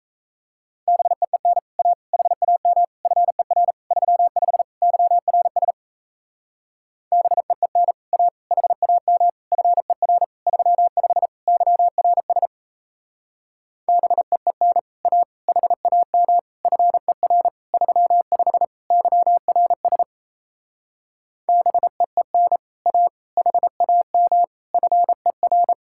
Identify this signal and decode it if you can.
Morse code, heard as a keyed tone, about 35 wpm, 700 Hz. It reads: BEEN A HAM FER 35 YRS BEEN A HAM FER 35 YRS BEEN A HAM FER 35 YRS BEEN A HAM FER